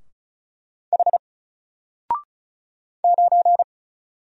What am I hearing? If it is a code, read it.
Morse code, heard as a keyed tone, about 35 wpm, 700 Hz. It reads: H 9
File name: Single-Letter-Number-35wpm